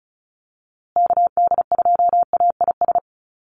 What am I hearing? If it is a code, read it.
Morse code, heard as a keyed tone, about 34 wpm, 700 Hz. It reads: KD2AIS
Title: USA Call Signs Repeated 2x 35wpm